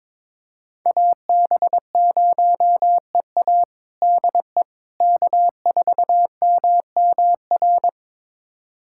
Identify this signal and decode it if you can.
Morse code, heard as a keyed tone, about 22 wpm, 700 Hz. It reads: AB0EA DE K4MMR
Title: USA Callsign DE Callsign - 5 Characters - 22wpm